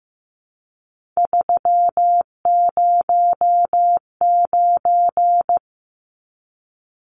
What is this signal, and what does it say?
Morse code, heard as a keyed tone, about 15 wpm, 700 Hz. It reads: 309
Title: Numbers - 3-Digit 15wpm